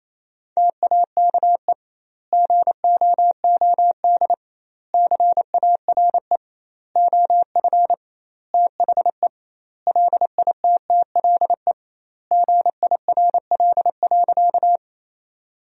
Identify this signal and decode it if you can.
Morse code, heard as a keyed tone, about 28 wpm, 700 Hz. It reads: TAKE GOOD CARE OF THE LITTLE GIRL.
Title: 2x Word-Spacing - Sentences from Top 400 Words - 28wpm